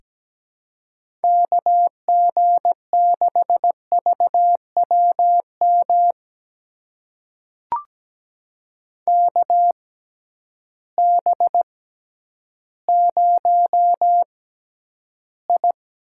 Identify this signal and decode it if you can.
Morse code, heard as a keyed tone, about 17 wpm, spaced as for 10 wpm, 700 Hz. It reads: KG6VWM KB0I
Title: USA Call Signs - 6 Characters - 6x Character Spacing - 17wpm